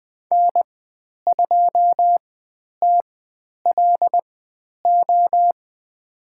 Morse code, heard as a keyed tone, about 20 wpm, spaced as for 10 wpm, 700 Hz. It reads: N2TLO
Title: 109 Lesson 18g - Letter m - Call Signs 20wpm